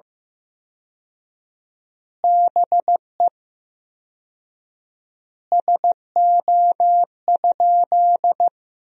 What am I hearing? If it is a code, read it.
Morse code, heard as a keyed tone, about 15 wpm, 700 Hz. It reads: BE SO?
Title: 4x Word-Spacing - Sentences from Top 300 Words - 15wpm